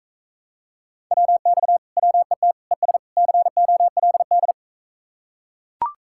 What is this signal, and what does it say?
Morse code, heard as a keyed tone, about 44 wpm, 700 Hz. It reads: WX WET ES COLD